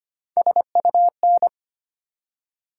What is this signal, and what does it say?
Morse code, heard as a keyed tone, about 25 wpm, 700 Hz. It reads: SUN
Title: Three Letter Words - 25wpm